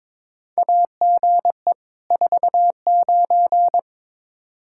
Morse code, heard as a keyed tone, about 22 wpm, 700 Hz. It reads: AGE 49
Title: Practice QSO - Exchange 3 - More Info - 22wpm